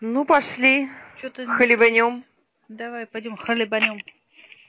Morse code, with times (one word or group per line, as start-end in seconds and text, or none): none